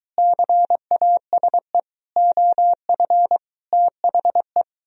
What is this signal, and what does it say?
Morse code, heard as a keyed tone, about 23 wpm, 700 Hz. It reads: CASE OF THE